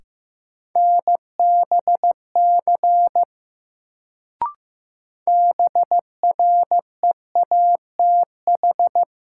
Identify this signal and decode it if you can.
Morse code, heard as a keyed tone, about 15 wpm, 700 Hz. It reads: NBC BREATH
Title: Sets of 5 Words Encore - New Word Drill - 15wpm